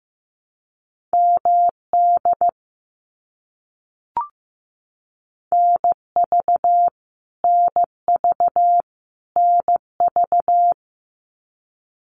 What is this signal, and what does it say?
Morse code, heard as a keyed tone, about 15 wpm, 700 Hz. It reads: MD NV NV NV
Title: US States Repeated 3x 15wpm